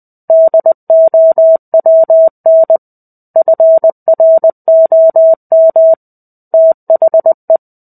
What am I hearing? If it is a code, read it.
Morse code, heard as a keyed tone, about 20 wpm, 650 Hz. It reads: DOWN FROM THE
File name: 05-Chapter-05 20 WPM